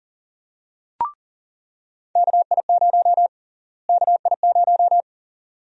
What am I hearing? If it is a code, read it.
Morse code, heard as a keyed tone, about 40 wpm, 700 Hz. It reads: KI0 KI0
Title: USA Call Sign Prefixes - Repeated 2x - 40wpm